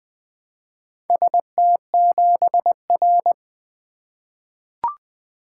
Morse code, heard as a keyed tone, about 20 wpm, 700 Hz. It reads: ST7R
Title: Four Letter-Number Combinations - 20wpm